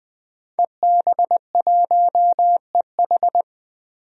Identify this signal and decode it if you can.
Morse code, heard as a keyed tone, about 21 wpm, 700 Hz. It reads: EB1EH
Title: DXCC-100 Call Signs - 5 Characters - 20wpm